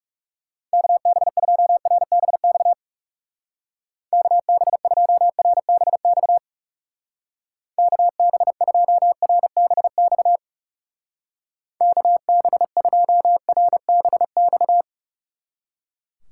Morse code, heard as a keyed tone, about 39 wpm, 700 Hz. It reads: KB2RBX KB2RBX KB2RBX KB2RBX